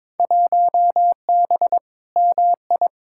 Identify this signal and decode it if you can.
Morse code, heard as a keyed tone, about 21 wpm, 700 Hz. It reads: 1B MI
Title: Field Day Exchanges - 22wpm